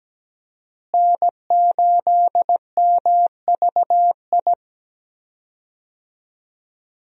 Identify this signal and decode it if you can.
Morse code, heard as a keyed tone, about 17 wpm, 700 Hz. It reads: N8MVI
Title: USA Call Signs - 5 Characters - Repeated 2x - 17wpm